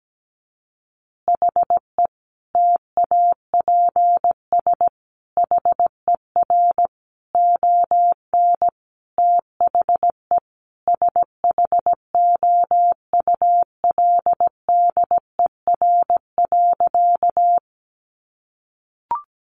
Morse code, heard as a keyed tone, about 17 wpm, 700 Hz. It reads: HE TAPS HER ON THE SHOULDER.
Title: Additional Easy Phrases 17wpm